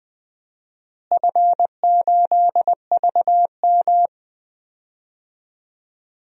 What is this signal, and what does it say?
Morse code, heard as a keyed tone, about 20 wpm, 700 Hz. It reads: F8VM
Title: DXCC-100 Call Signs - 4 Characters - Repeated 3x - 20wpm